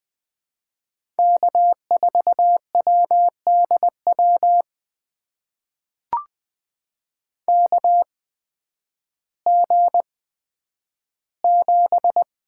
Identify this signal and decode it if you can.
Morse code, heard as a keyed tone, about 20 wpm, spaced as for 11 wpm, 700 Hz. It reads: K4WDW KG7
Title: USA Call Signs - 8x Character Spacing - 20wpm